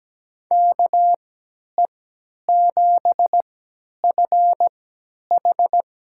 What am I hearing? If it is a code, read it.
Morse code, heard as a keyed tone, about 17 wpm, spaced as for 10 wpm, 700 Hz. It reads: KE7FH
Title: USA Call Signs - 3x Character Spacing - 17wpm